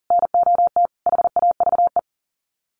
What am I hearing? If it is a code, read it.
Morse code, heard as a keyed tone, about 37 wpm, 700 Hz. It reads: NOT HAVE
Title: Sets of 5 Words Encore 40wpm